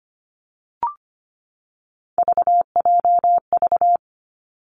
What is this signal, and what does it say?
Morse code, heard as a keyed tone, about 25 wpm, 700 Hz. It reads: VJV
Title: Three Challenging Letter-Number Combinations 25wpm